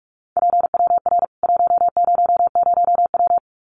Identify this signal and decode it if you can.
Morse code, heard as a keyed tone, about 44 wpm, 700 Hz. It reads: PWR 100W
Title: Common QSO Phrases 45wpm